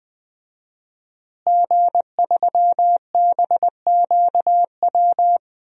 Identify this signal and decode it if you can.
Morse code, heard as a keyed tone, about 20 wpm, 700 Hz. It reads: G3BQW
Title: DXCC-100 Call Signs - 5 Characters - Repeated 2x - 20wpm